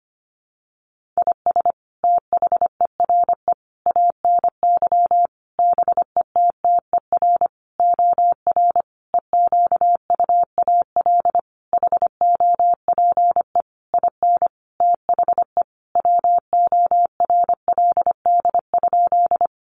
Morse code, heard as a keyed tone, about 25 wpm, 700 Hz. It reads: IS THERE ANY BETTER OR EQUAL HOPE IN THE WORLD?
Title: Sentences from Top 1400 Words 25wpm